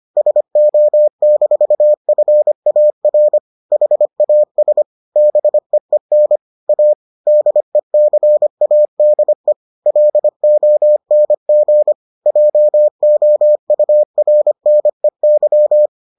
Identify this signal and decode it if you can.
Morse code, heard as a keyed tone, about 25 wpm, 600 Hz. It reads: SO-FAR HAS BEEN A DECADE LONG JOURNEY